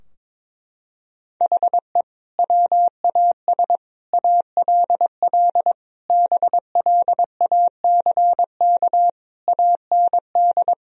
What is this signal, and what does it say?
Morse code, heard as a keyed tone, about 22 wpm, 700 Hz. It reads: HE WAS ALL BLACK AND
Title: Sentences from Top 1000 Words - Review Part 3 - 22wpm